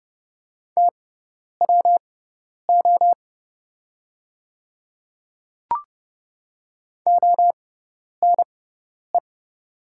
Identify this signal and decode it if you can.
Morse code, heard as a keyed tone, about 31 wpm, spaced as for 10 wpm, 700 Hz. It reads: TWO ONE
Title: 6x Character Spacing - Top 100 Words - 30wpm